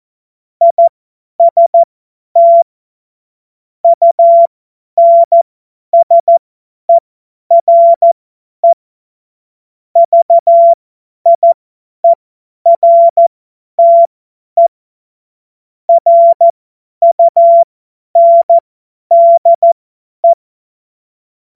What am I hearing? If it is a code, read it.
Morse code, heard as a keyed tone, about 14 wpm, spaced as for 10 wpm, 680 Hz. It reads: IST UNSERE VIERTE RUNDE
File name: Morsekurs / Witze / Witz 25. April 2023 50 F70